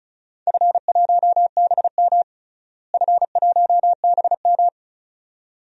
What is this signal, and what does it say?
Morse code, heard as a keyed tone, about 34 wpm, 700 Hz. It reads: F1BM F1BM
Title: DXCC-100 Call Signs - 4 Characters - Repeated 2x - 35wpm